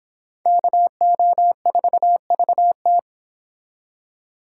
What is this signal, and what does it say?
Morse code, heard as a keyed tone, about 26 wpm, 700 Hz. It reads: KO4VT
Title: Speed Racing - USA 2x2 Call Signs 22wpm